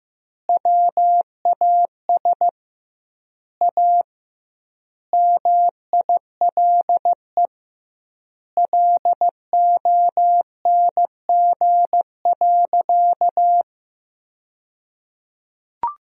Morse code, heard as a keyed tone, about 15 wpm, 700 Hz. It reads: WAS A MILE LONG.
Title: 2x Word-Spacing - Sentences from Top 500 Words - 15wpm